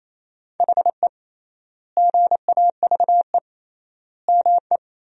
Morse code, heard as a keyed tone, about 28 wpm, 700 Hz. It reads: HE GAVE ME
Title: 3x Word-Spacing - Sentences from Top 500 Words - 28wpm